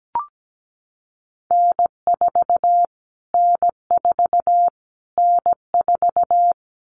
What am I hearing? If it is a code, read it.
Morse code, heard as a keyed tone, about 17 wpm, 700 Hz. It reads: N4 N4 N4
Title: Callsign Prefixes for Top 25 DXCC - Repeated 3x - 17wpm